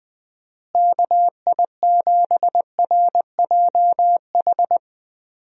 Morse code, heard as a keyed tone, about 20 wpm, 700 Hz. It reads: KI7RJH
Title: USA Call Signs - 6 Characters - Repeated 3x - 20wpm